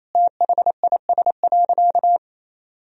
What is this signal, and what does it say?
Morse code, heard as a keyed tone, about 28 wpm, 700 Hz. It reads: THIS.
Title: Sentences from Top 1000 Words - Review Part 2 - 28wpm